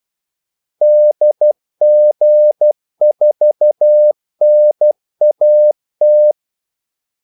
Morse code, heard as a keyed tone, about 12 wpm, 600 Hz. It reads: DG4NAT
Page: 2 ekstra ordmellemrum mellem kaldesignalerne